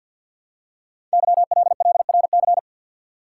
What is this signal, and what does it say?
Morse code, heard as a keyed tone, about 48 wpm, 700 Hz. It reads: YLLRC
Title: Code Groups - 5 Letters - 50wpm